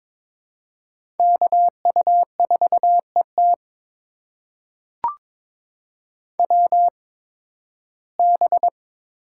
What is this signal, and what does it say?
Morse code, heard as a keyed tone, about 22 wpm, spaced as for 14 wpm, 700 Hz. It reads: KU4ET WB
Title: USA Call Signs - 8x Character Spacing - 22wpm